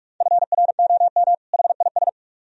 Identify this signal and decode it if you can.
Morse code, heard as a keyed tone, about 44 wpm, 700 Hz. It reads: FROM HIS